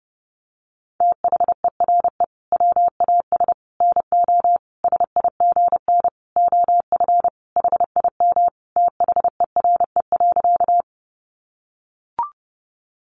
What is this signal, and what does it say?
Morse code, heard as a keyed tone, about 30 wpm, 700 Hz. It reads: THERE WAS NO SIGN OF HIM THERE.
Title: Sentences from Top 1000 Words 30wpm